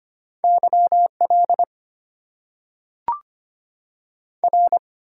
Morse code, heard as a keyed tone, about 25 wpm, 690 Hz. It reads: YL R
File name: Top 90 QSO Elements - 8x Character Spacing - 25wpm